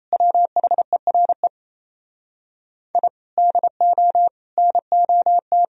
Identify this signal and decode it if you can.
Morse code, heard as a keyed tone, about 29 wpm, 700 Hz. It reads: WHERE I DO NOT
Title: Speed-Racing - Sets of 5 Words 28wpm